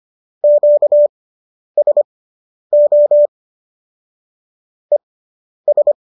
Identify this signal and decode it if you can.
Morse code, heard as a keyed tone, about 25 wpm, spaced as for 10 wpm, 590 Hz. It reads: QSO ES